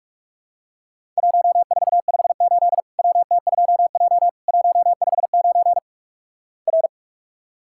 Morse code, heard as a keyed tone, about 46 wpm, 700 Hz. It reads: 1458 WT2J 159 R
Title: Worked All Europe DX Contest - QTC Exchange - 45wpm